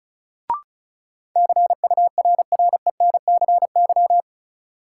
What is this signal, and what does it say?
Morse code, heard as a keyed tone, about 36 wpm, 700 Hz. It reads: CURRENCY
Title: Sentences from Top 1500 Words - New Word Drill - 35wpm